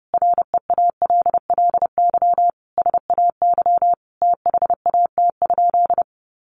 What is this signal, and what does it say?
Morse code, heard as a keyed tone, about 30 wpm, 700 Hz. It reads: REALLY SAY THAT?
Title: Sentences from Top 1500 Words 30wpm